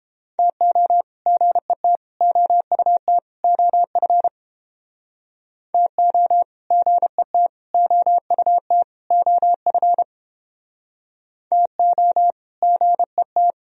Morse code, heard as a keyed tone, about 30 wpm, 700 Hz. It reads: TO GET OUT OF TO GET OUT OF TO GET